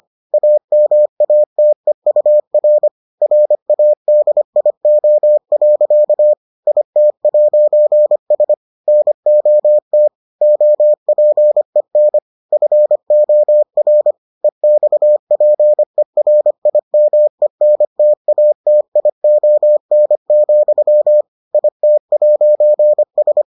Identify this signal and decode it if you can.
Morse code, heard as a keyed tone, about 25 wpm, 600 Hz. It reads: AMATEUR RADIO. IT'S NOT OPEN FOR EXPERIMENTATION, IT'S